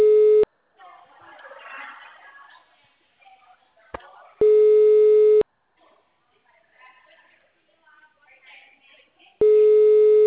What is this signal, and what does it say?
Ringback tone (Europe, 425 Hz), with speech.